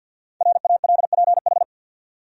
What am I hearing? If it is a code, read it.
Morse code, heard as a keyed tone, about 48 wpm, 700 Hz. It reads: RALPH
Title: USA Common Names - 50wpm